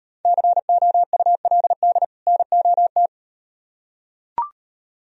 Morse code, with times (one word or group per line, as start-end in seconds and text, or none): 0.3-2.0 COULD
2.3-3.1 NOT